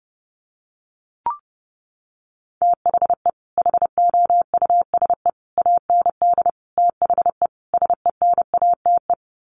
Morse code, heard as a keyed tone, about 30 wpm, 700 Hz. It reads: THE HOUSE AND THE SENATE
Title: Sets of 5 Words Encore2 30wpm